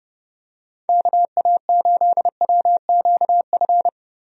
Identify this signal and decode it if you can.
Morse code, heard as a keyed tone, about 32 wpm, 700 Hz. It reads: KA8WQF
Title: USA Call Signs - 6 Characters - 30wpm